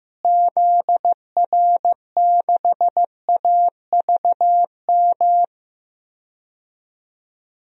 Morse code, heard as a keyed tone, about 15 wpm, 700 Hz. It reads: ZR6AVM
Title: DXCC-100 Call Signs - 6 Characters - Repeated 3x - 15wpm